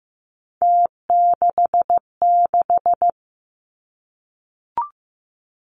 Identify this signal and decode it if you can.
Morse code, heard as a keyed tone, about 15 wpm, 700 Hz. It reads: T66
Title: Three Challenging Letter-Number Combinations 15wpm